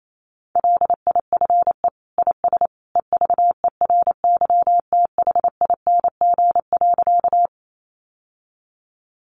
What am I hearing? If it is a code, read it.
Morse code, heard as a keyed tone, about 28 wpm, 700 Hz. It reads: LIFE IS EVERYTHING.
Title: Sentences from Top 700 Words 28wpm